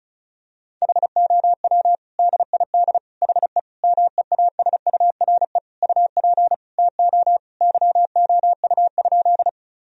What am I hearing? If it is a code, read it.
Morse code, heard as a keyed tone, about 35 wpm, 700 Hz. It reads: HOW DID HE MEASURE UP TO YOU?
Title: Sentences from Top 1000 Words - Review Part 2 - 35wpm